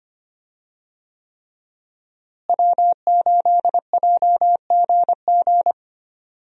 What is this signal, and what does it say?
Morse code, heard as a keyed tone, about 25 wpm, 700 Hz. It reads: W8JGG
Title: USA Call Signs - 5 Characters - Repeated 3x - 25wpm